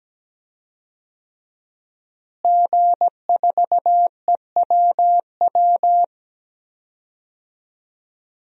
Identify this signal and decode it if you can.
Morse code, heard as a keyed tone, about 17 wpm, 700 Hz. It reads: G4EWW
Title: DXCC-100 Call Signs - 5 Characters - Repeated 2x - 17wpm